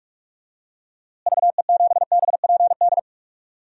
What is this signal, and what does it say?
Morse code, heard as a keyed tone, about 44 wpm, 700 Hz. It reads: VE7BPD
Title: DXCC-100 Call Signs - 6 Characters - Repeated 3x - 45wpm